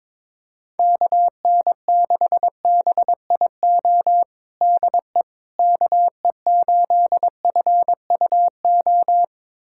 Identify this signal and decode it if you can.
Morse code, heard as a keyed tone, about 22 wpm, 700 Hz. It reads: KN6BIO DE KE8FUO
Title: USA Callsign DE Callsign - 6 Characters - 22wpm